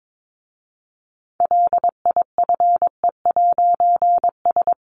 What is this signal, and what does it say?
Morse code, heard as a keyed tone, about 22 wpm, 700 Hz. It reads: LIFE'S
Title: A Sound Of Thunder 22wpm